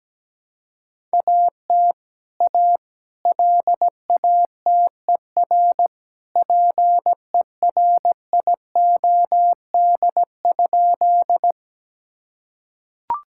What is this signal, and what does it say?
Morse code, heard as a keyed tone, about 17 wpm, 700 Hz. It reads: AT A LATER PERIOD?
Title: Sentences from Top 1000 Words - Review Part 5 - 17wpm